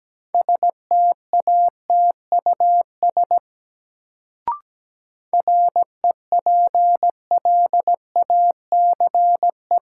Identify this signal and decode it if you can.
Morse code, heard as a keyed tone, about 17 wpm, 700 Hz. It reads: STATUS REPLACE
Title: Sentences from Top 1400 Words - New Word Drill Alt - 17wpm